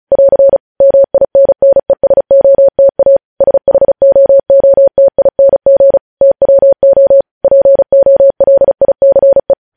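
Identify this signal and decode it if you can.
Morse code, heard as a keyed tone, about 36 wpm, 550 Hz. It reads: <AR> MINNESOTA SHOOTING TWO POLICE
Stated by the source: Morse code news headlines sent at 35 words per minute.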